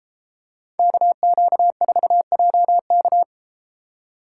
Morse code, heard as a keyed tone, about 30 wpm, 700 Hz. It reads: KQ4JK